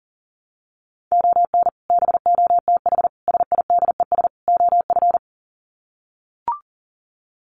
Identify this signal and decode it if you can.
Morse code, heard as a keyed tone, about 40 wpm, 700 Hz. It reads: ON BOTH SIDES OF
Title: Sets of 4 Words 40wpm